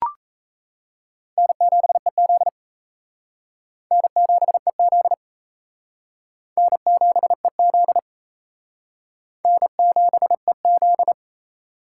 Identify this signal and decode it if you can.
Morse code, heard as a keyed tone, about 36 wpm, 700 Hz. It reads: N7EZ N7EZ N7EZ N7EZ